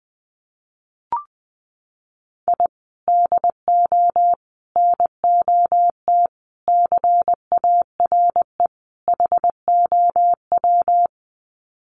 Morse code, heard as a keyed tone, about 20 wpm, 700 Hz. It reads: I DO NOT CARE HOW
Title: Sets of 5 Words Encore2 20wpm